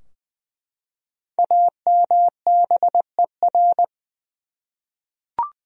Morse code, heard as a keyed tone, about 20 wpm, 700 Hz. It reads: AMBER
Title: USA Common Names - Max 5 Letters - 20wpm